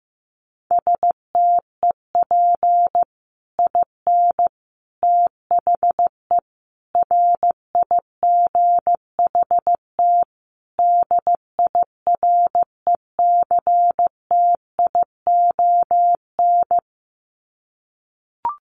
Morse code, heard as a keyed tone, about 15 wpm, 700 Hz. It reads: STEP IN THE RIGHT DIRECTION
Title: Sets of 5 Words Encore 15wpm